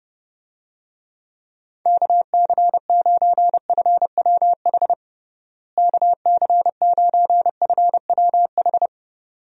Morse code, heard as a keyed tone, about 32 wpm, 700 Hz. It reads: KC9FWH KC9FWH
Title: USA Call Signs - 6 Characters - Repeated 2x - 30wpm